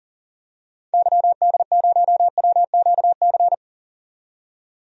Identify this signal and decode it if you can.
Morse code, heard as a keyed tone, about 40 wpm, 700 Hz. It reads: YD0WQC
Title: DXCC-100 Call Signs - 6 Characters - 40wpm